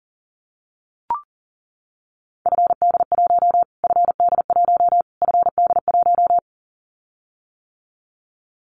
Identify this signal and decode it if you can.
Morse code, heard as a keyed tone, about 37 wpm, 700 Hz. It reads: FD1 FD1 FD1
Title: Callsign Prefixes for Top 25 DXCC - Repeated 3x - 40wpm